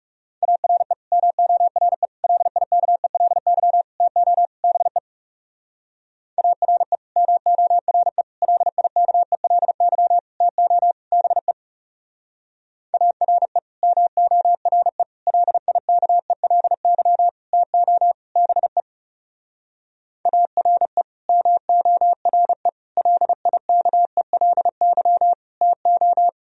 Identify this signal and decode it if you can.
Morse code, heard as a keyed tone, about 38 wpm, 700 Hz. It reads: ARE MORE LIKELY TO BE ARE MORE LIKELY TO BE ARE MORE LIKELY TO BE ARE MORE LIKELY TO